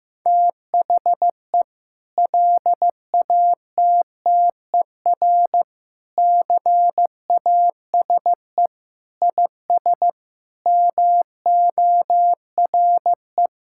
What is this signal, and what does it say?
Morse code, heard as a keyed tone, about 15 wpm, 700 Hz. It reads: THE LATTER CASE IS MORE
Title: Sentences from Top 1600 Words 15wpm